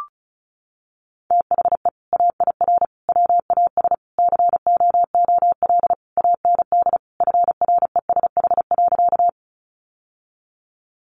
Morse code, heard as a keyed tone, about 34 wpm, 700 Hz. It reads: THE AIR WAS COOL AND FRESH.
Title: Sentences from Top 1200 Words 35wpm